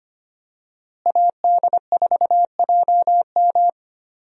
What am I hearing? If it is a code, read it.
Morse code, heard as a keyed tone, about 25 wpm, 700 Hz. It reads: AD4JM
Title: USA Call Signs - 5 Characters - Repeated 3x - 25wpm